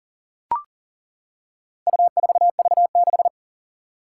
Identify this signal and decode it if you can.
Morse code, heard as a keyed tone, about 40 wpm, 700 Hz. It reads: U4V6
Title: Four Challenging Letter-Number Combinations 40wpm